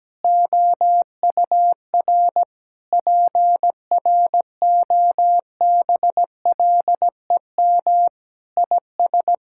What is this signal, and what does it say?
Morse code, heard as a keyed tone, about 17 wpm, 700 Hz. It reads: OUR PROBLEM IS
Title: Sentences from Top 1600 Words 17wpm